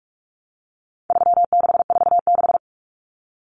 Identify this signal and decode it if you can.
Morse code, heard as a keyed tone, about 44 wpm, 700 Hz. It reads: 3646
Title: Common Cut Numbers - 4-Digit 45wpm